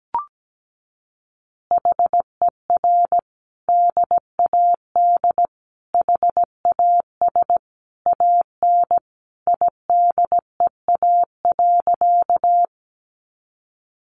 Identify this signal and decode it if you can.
Morse code, heard as a keyed tone, about 17 wpm, 700 Hz. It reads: HER DAD HAS AN IDEA.
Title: Additional Easy Phrases 17wpm